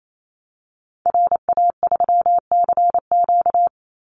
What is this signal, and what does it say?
Morse code, heard as a keyed tone, about 28 wpm, 700 Hz. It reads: RA3CQ
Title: Call Signs - 28wpm